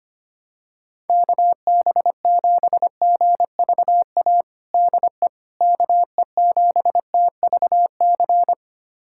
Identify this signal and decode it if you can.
Morse code, heard as a keyed tone, about 25 wpm, 700 Hz. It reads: KB7GVA DE KE7TVC
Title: USA Callsign DE Callsign - 6 Characters - 25wpm